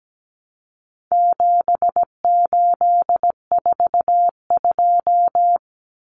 Morse code, heard as a keyed tone, about 17 wpm, 700 Hz. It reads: 7842
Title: Numbers - 4-Digit 17wpm